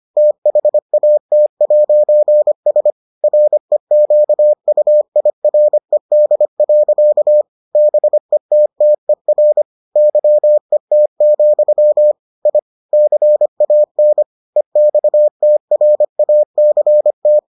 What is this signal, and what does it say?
Morse code, heard as a keyed tone, about 25 wpm, 600 Hz. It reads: THAT'S REQUIRED. BETTER YET, I CAN EXTRACT